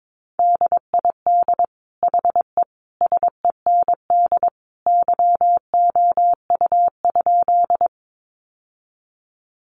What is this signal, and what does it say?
Morse code, heard as a keyed tone, about 22 wpm, 700 Hz. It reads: DID HE SEND YOU?
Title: Sentences from Top 1500 Words 22wpm